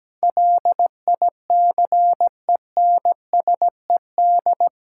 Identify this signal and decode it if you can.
Morse code, heard as a keyed tone, about 17 wpm, 700 Hz. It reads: LICENSED
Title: Practice QSO - Exchange 3 - More Info - 17wpm